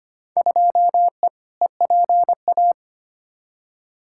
Morse code, heard as a keyed tone, about 25 wpm, 700 Hz. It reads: 2E EPA
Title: Field Day Exchanges Encore - 25wpm